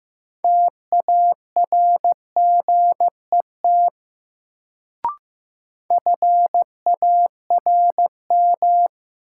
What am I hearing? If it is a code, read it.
Morse code, heard as a keyed tone, about 15 wpm, 700 Hz. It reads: TARGET FARM
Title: Sentences from Top 1200 Words - New Word Drill Alt - 15wpm